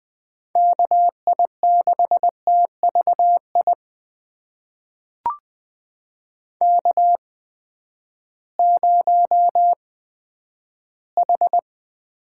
Morse code, heard as a keyed tone, about 20 wpm, spaced as for 11 wpm, 700 Hz. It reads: KI6TVI K0H
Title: USA Call Signs - 8x Character Spacing - 20wpm